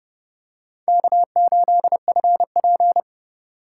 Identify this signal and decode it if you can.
Morse code, heard as a keyed tone, about 30 wpm, 700 Hz. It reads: K8FP
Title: USA Call Signs - 4 Characters - 2x Character Spacing - 30wpm